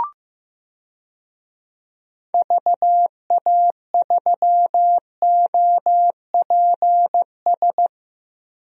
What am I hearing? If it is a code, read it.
Morse code, heard as a keyed tone, about 15 wpm, 700 Hz. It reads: VA3OPS
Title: DXCC-100 Call Signs - 6 Characters - Repeated 2x - 15wpm